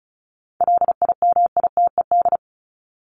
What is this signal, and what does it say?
Morse code, heard as a keyed tone, about 34 wpm, 700 Hz. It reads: LIMITED
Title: ED Suffix - Top 100 Words with Base Words 35wpm